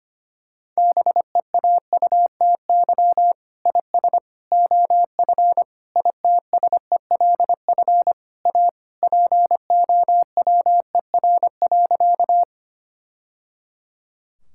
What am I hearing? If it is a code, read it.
Morse code, heard as a keyed tone, about 25 wpm, 700 Hz. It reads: BEAUTY IS OF ITSELF A POWER.
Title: Sentences from Top 1000 Words - Review Part 5 - 25wpm